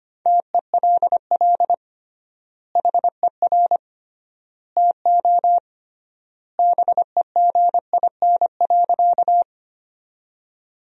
Morse code, heard as a keyed tone, about 25 wpm, 700 Hz. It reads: TELL HER TO BEGIN.
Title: 3x Word-Spacing - Sentences from Top 300 Words - 25wpm